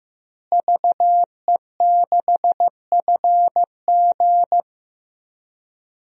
Morse code, heard as a keyed tone, about 15 wpm, 700 Hz. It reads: VE6FG
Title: DXCC-100 Call Signs - 5 Characters - Repeated 3x - 15wpm